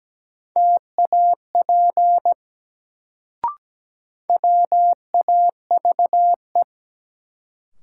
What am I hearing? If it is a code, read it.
Morse code, heard as a keyed tone, about 17 wpm, 700 Hz. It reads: TAP WAVE
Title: Sentences from Top 1400 Words - New Word Drill - 17wpm